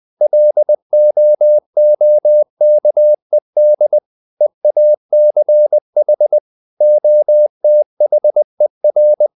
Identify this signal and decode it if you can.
Morse code, heard as a keyed tone, about 20 wpm, 600 Hz. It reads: LOOKED EACH OTHER